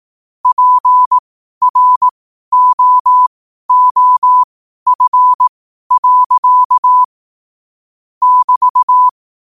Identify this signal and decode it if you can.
Morse code, heard as a keyed tone, about 18 wpm, spaced as for 12 wpm, 1000 Hz. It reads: PROOF. <BT>
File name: qotdfiles / 12wpm